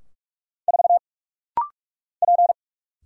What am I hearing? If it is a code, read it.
Morse code, heard as a keyed tone, about 44 wpm, 700 Hz. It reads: 4 P
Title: Single Letter-Number - Rapid-Fire 45wpm